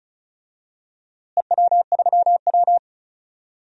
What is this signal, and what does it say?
Morse code, heard as a keyed tone, about 36 wpm, 700 Hz. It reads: EW3W
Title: DXCC-100 Call Signs - 4 Characters - Repeated 2x - 35wpm